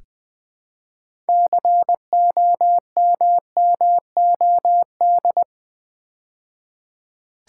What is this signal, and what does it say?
Morse code, heard as a keyed tone, about 20 wpm, 700 Hz. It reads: COMMOD
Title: Build A Word - 3 Letters At A Time - 9-Letter Words - 20wpm